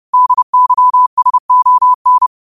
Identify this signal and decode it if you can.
Morse code, heard as a keyed tone, about 30 wpm, 1000 Hz. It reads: DYSON
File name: qotdfiles / 30wpm